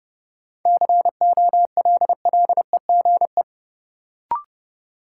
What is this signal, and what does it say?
Morse code, heard as a keyed tone, about 29 wpm, 700 Hz. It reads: COLLEGE
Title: Sentences from Top 1300 Words - New Word Drill Alt - 30wpm